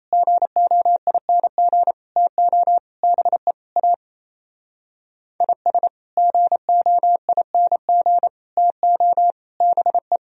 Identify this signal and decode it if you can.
Morse code, heard as a keyed tone, about 36 wpm, 700 Hz. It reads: GOING TO BE A IS GOING TO BE